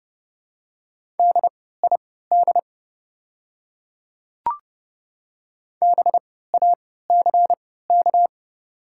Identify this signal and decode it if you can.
Morse code, heard as a keyed tone, about 30 wpm, spaced as for 17 wpm, 700 Hz. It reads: DID BACK
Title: 3x Character Spacing - Top 200 Words - 30wpm